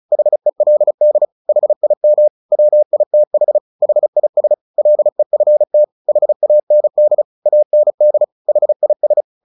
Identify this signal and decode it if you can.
Morse code, heard as a keyed tone, about 34 wpm, 600 Hz. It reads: HELD HIM WITH HIS LEFT HAND AND HIS
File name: the old man and the sea 2 35wpm